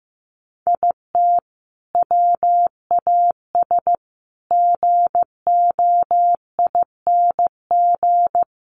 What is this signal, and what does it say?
Morse code, heard as a keyed tone, about 15 wpm, 700 Hz. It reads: IT WAS GOING
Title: Sets of 4 Words 15wpm